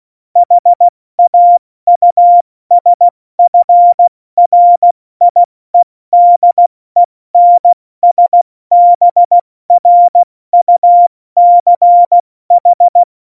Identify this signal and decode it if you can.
Morse code, heard as a keyed tone, about 16 wpm, 700 Hz. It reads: HAUSFRIEDENSBRUCH
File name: Morsekurs / Witze / Witz 05. Januar 2021, 70 F80